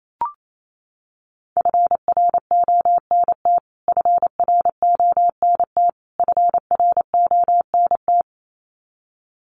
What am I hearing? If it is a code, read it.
Morse code, heard as a keyed tone, about 28 wpm, 700 Hz. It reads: FRONT FRONT FRONT
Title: Top 500 Words Repeated 28wpm